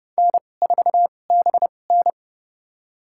Morse code, heard as a keyed tone, about 30 wpm, spaced as for 22 wpm, 700 Hz. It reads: N4BN
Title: USA Call Signs - 2x Character Spacing - 30wpm